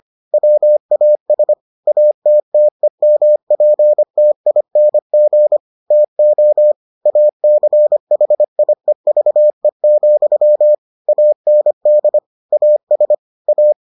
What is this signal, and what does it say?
Morse code, heard as a keyed tone, about 25 wpm, 600 Hz. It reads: WAS ATTEMPTING TO ACHIEVE, AND AS A